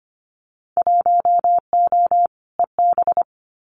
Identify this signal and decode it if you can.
Morse code, heard as a keyed tone, about 25 wpm, 700 Hz. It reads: 1O EB
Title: Winter Field Day Exchanges 25wpm